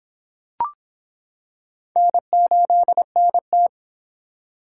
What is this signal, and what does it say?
Morse code, heard as a keyed tone, about 25 wpm, 700 Hz. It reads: N8NT